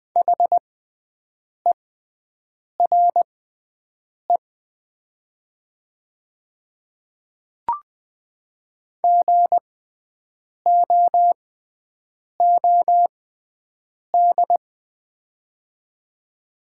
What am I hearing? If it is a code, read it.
Morse code, heard as a keyed tone, about 20 wpm, spaced as for 7 wpm, 700 Hz. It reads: HERE GOOD
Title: 6x Character Spacing - Top 300 Words - 20wpm